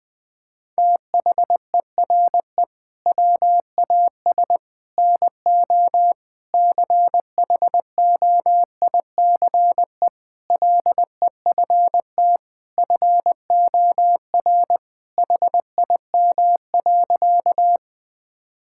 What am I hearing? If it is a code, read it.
Morse code, heard as a keyed tone, about 21 wpm, 700 Hz. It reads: THERE WAS NO CHOICE LEFT FOR HIM.
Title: Sentences from Top 1000 Words - Review Part 6 - 20wpm